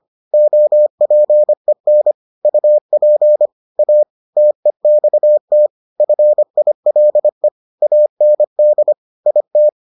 Morse code, heard as a keyed tone, about 25 wpm, 600 Hz. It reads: OPEN UP A TEXT FILE AND IT